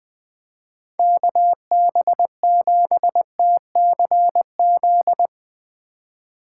Morse code, heard as a keyed tone, about 20 wpm, 700 Hz. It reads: KB7TCZ